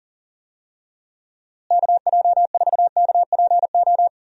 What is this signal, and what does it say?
Morse code, heard as a keyed tone, about 40 wpm, 700 Hz. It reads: KJ4KPO
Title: USA Call Signs - 6 Characters - Repeated 2x - 40wpm